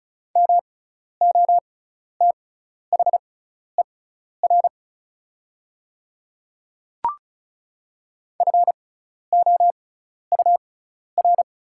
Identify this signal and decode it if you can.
Morse code, heard as a keyed tone, about 36 wpm, spaced as for 12 wpm, 690 Hz. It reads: MOTHER FOUR
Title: 6x Character Spacing - Top 300 Words - 35wpm